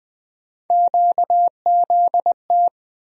Morse code, heard as a keyed tone, about 20 wpm, 700 Hz. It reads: QZT
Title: USA Call Sign Postfixes - 3 Letters 20wpm